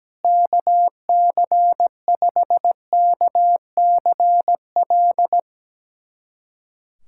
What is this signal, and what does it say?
Morse code, heard as a keyed tone, about 17 wpm, 700 Hz. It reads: KC5KCL
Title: USA Callsign DE Callsign 17wpm